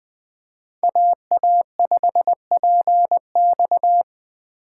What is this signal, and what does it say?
Morse code, heard as a keyed tone, about 24 wpm, 700 Hz. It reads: AA5PX
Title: Speed Racing - USA 2x2 Call Signs 20wpm